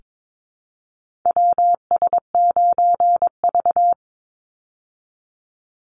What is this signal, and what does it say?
Morse code, heard as a keyed tone, about 22 wpm, 690 Hz. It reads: WS9V
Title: Call Signs Repeated 3x - 22wpm